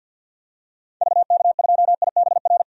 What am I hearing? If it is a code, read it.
Morse code, heard as a keyed tone, about 51 wpm, 700 Hz. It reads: VK2IBR
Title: DXCC-100 Call Signs - 6 Characters - 50wpm